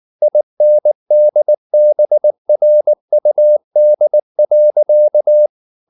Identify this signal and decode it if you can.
Morse code, heard as a keyed tone, about 19 wpm, 600 Hz. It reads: INDBRUD.
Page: Ferie 19 wpm